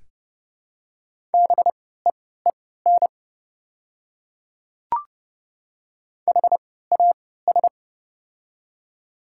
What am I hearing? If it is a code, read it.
Morse code, heard as a keyed tone, about 30 wpm, spaced as for 17 wpm, 700 Hz. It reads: BEEN HAS
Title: 3x Character Spacing - Top 100 Words - 30wpm